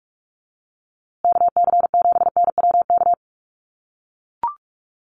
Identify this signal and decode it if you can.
Morse code, heard as a keyed tone, about 44 wpm, 700 Hz. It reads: KC7NWK
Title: USA Call Signs Repeated 3x 45wpm